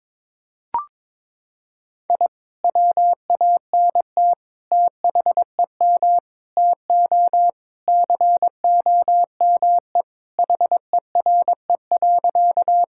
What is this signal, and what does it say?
Morse code, heard as a keyed tone, about 22 wpm, 700 Hz. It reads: I WANT THEM TO COME HERE.
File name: practice / 22wpm / Sentences with 2-Word Phrases - 22wpm